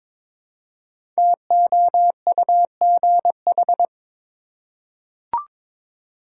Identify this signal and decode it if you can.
Morse code, heard as a keyed tone, about 22 wpm, 700 Hz. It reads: TOUGH
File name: 5 Letter Words - First 3 Letters Given - 22wpm